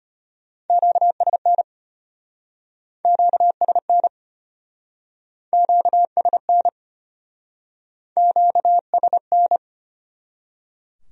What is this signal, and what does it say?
Morse code, heard as a keyed tone, about 33 wpm, 700 Hz. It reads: QSN QSN QSN QSN